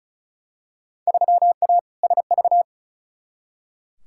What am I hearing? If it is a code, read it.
Morse code, heard as a keyed tone, about 34 wpm, 700 Hz. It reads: 3A SV
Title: Field Day Exchanges - 35wpm